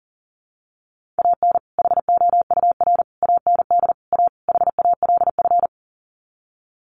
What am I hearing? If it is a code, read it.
Morse code, heard as a keyed tone, about 40 wpm, 700 Hz. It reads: AN HOUR AND A HALF
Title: Sets of 5 Words 40wpm